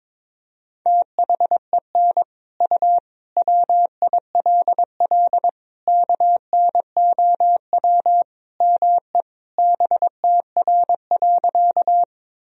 Morse code, heard as a keyed tone, about 22 wpm, 700 Hz. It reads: THEN U WILL KNOW ME BTR.
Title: Sentences from Top 400 Words and Abbreviations 22wpm